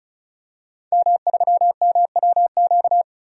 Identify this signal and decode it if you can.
Morse code, heard as a keyed tone, about 36 wpm, 700 Hz. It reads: M3MWQ
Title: DXCC-100 Call Signs - 5 Characters - 35wpm